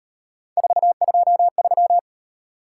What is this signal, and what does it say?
Morse code, heard as a keyed tone, about 38 wpm, 700 Hz. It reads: 423